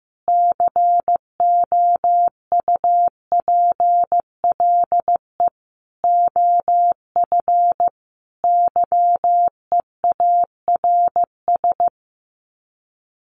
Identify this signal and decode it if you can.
Morse code, heard as a keyed tone, about 15 wpm, 700 Hz. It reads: COUPLE OF YEARS
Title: Sets of 4 Words 15wpm